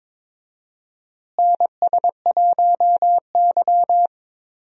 Morse code, heard as a keyed tone, about 22 wpm, 700 Hz. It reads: NS1Y
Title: USA Call Signs - 4 Characters - Repeated 3x - 22wpm